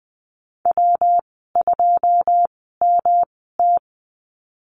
Morse code, heard as a keyed tone, about 20 wpm, spaced as for 15 wpm, 700 Hz. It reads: W2MT
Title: 128 Lesson 21g - Number 6 - Call Signs 20wpm